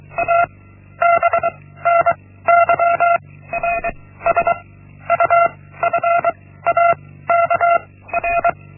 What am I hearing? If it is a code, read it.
Morse code, heard as a keyed tone, about 23 wpm, 680 Hz. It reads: A B N Y R S U F A K R
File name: Linksmakalnis-1